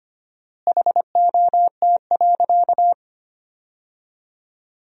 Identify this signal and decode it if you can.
Morse code, heard as a keyed tone, about 25 wpm, 700 Hz. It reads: HOT.
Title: 4x Word-Spacing - Sentences from Top 500 Words - 25wpm